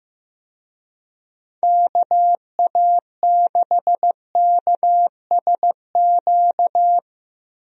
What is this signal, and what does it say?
Morse code, heard as a keyed tone, about 16 wpm, 700 Hz. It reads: KA6KSQ
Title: USA Call Signs - 6 Characters - Repeated 3x - 15wpm